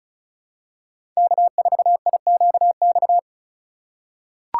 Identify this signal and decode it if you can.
Morse code, heard as a keyed tone, about 36 wpm, 700 Hz. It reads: K4IQX
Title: USA Call Signs - 5 Characters - 3x Character Spacing - 35wpm